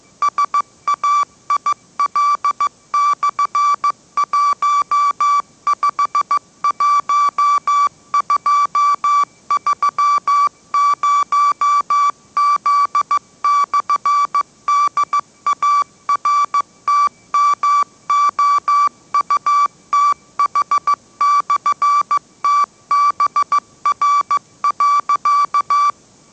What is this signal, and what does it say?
Morse code, heard as a keyed tone, about 15 wpm, 1160 Hz. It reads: SAIL/151230Z/DARTMOUTH/TBR.